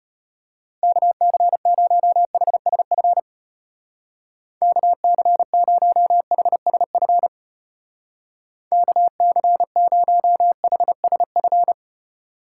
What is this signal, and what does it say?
Morse code, heard as a keyed tone, about 32 wpm, 700 Hz. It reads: KC0HSF KC0HSF KC0HSF